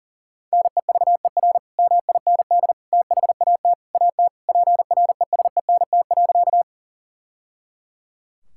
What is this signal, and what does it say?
Morse code, heard as a keyed tone, about 37 wpm, 700 Hz. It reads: NEVER MIND THAT AT PRESENT.
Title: Sentences from Top 1000 Words - Review Part 5 - 40wpm